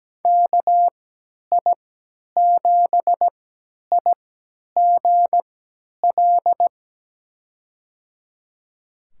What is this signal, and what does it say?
Morse code, heard as a keyed tone, about 17 wpm, spaced as for 10 wpm, 700 Hz. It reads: KI7IGL
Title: USA Call Signs - 3x Character Spacing - 17wpm